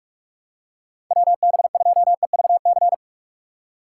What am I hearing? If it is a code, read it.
Morse code, heard as a keyed tone, about 46 wpm, 700 Hz. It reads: WB2EVC